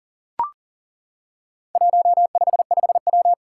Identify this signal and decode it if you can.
Morse code, heard as a keyed tone, about 40 wpm, 700 Hz. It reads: 155W
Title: Four Challenging Letter-Number Combinations 40wpm